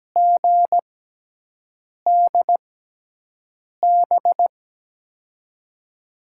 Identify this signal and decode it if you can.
Morse code, heard as a keyed tone, about 17 wpm, spaced as for 6 wpm, 700 Hz. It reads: GDB
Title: USA Call Signs - 6 Characters - 6x Character Spacing - 17wpm